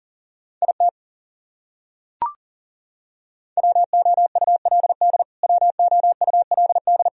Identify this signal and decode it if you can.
Morse code, heard as a keyed tone, about 40 wpm, 700 Hz. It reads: IT WOULD WOULD
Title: Top 100 Words Repeated 40wpm